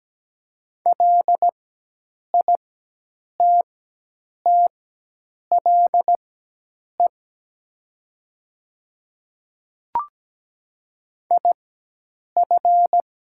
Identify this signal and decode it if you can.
Morse code, heard as a keyed tone, about 17 wpm, spaced as for 8 wpm, 690 Hz. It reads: LITTLE IF
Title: 4x Character Spacing - Top 200 Words - 17wpm